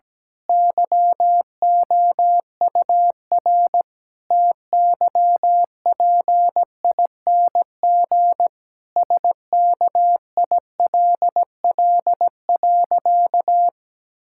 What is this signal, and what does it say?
Morse code, heard as a keyed tone, about 17 wpm, 700 Hz. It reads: YOUR TYPING SKILL.
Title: Pangram Challenge v2 - 17wpm